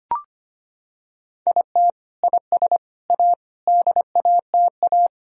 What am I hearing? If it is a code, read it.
Morse code, heard as a keyed tone, about 25 wpm, 700 Hz. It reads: IT IS A DATA
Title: Sentences from Top 1000 Words - Review Part 3 - 25wpm